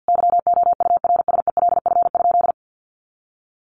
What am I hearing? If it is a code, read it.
Morse code, heard as a keyed tone, about 50 wpm, 700 Hz. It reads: YOURSELF?
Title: Sentences from Top 1200 Words 50wpm